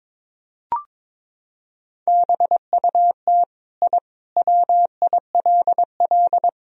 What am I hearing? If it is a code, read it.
Morse code, heard as a keyed tone, about 22 wpm, 700 Hz. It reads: BUT I WILL
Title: Sentences from Top 1000 Words - Review Part 3 - 22wpm